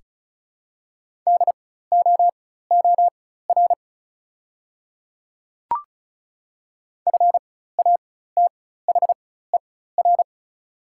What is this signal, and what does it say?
Morse code, heard as a keyed tone, about 36 wpm, spaced as for 16 wpm, 690 Hz. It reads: DOOR FATHER
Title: 4x Character Spacing - Top 500 Words - 35wpm